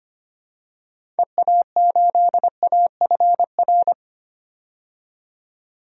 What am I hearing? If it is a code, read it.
Morse code, heard as a keyed tone, about 25 wpm, 700 Hz. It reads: EA8AFR
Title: DXCC-100 Call Signs - 6 Characters - 25wpm